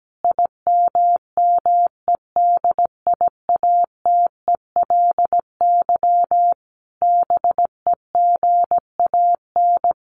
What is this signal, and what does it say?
Morse code, heard as a keyed tone, about 17 wpm, 700 Hz. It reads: IMMEDIATELY BEGAN
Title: Sentences from Top 1100 Words 17wpm